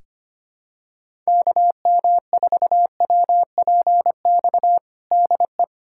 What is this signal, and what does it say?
Morse code, heard as a keyed tone, about 25 wpm, 700 Hz. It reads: KM4WPX DE
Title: USA Callsign DE Callsign - 6 Characters - 25wpm